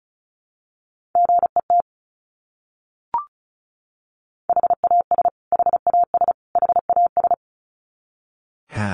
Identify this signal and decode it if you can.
Morse code, heard as a keyed tone, about 34 wpm, 700 Hz. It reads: GET HAS HAS HAS
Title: Top 200 Words Repeated 35wpm